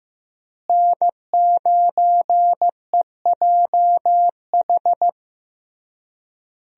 Morse code, heard as a keyed tone, about 15 wpm, 700 Hz. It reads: N9EJH
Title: USA Call Signs - 5 Characters - Repeated 3x - 15wpm